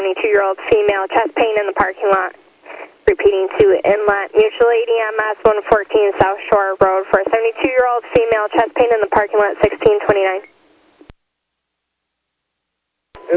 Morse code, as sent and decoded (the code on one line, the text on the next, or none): none